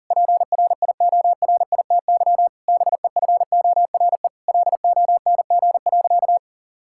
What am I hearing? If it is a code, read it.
Morse code, heard as a keyed tone, about 40 wpm, 700 Hz. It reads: PRIORITY BEFORE LONG.
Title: Sentences from Top 1600 Words 40wpm